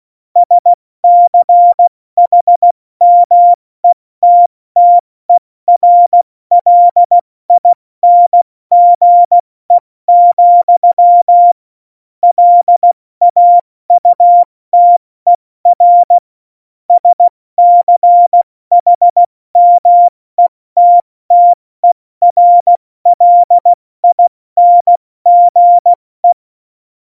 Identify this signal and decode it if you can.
Morse code, heard as a keyed tone, about 16 wpm, spaced as for 14 wpm, 700 Hz. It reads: SCHMETTERLINGE, LAUTER SCHMETTERLINGE
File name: Morsekurs / Witze / Witz 06. Februar 2018, 70 F80